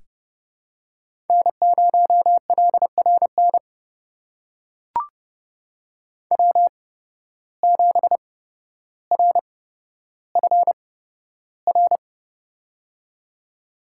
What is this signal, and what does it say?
Morse code, heard as a keyed tone, about 30 wpm, spaced as for 13 wpm, 700 Hz. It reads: N0LRN W7RFR
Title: USA Call Signs - 5 Characters - 8x Character Spacing - 30wpm